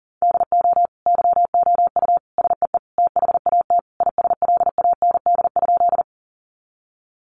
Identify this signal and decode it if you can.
Morse code, heard as a keyed tone, about 40 wpm, 700 Hz. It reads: DO YOU SEE THAT ISLAND?
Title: Sentences from Top 500 Words 40wpm